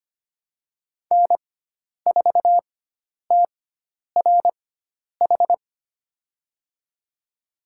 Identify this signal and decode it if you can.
Morse code, heard as a keyed tone, about 25 wpm, spaced as for 10 wpm, 700 Hz. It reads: N4TRH
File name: Lesson 16g - Letter c - Call Signs 25wpm